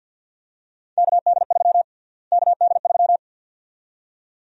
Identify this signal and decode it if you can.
Morse code, heard as a keyed tone, about 52 wpm, 700 Hz. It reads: KD3 KD3
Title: USA Call Sign Prefixes - Repeated 2x - 50wpm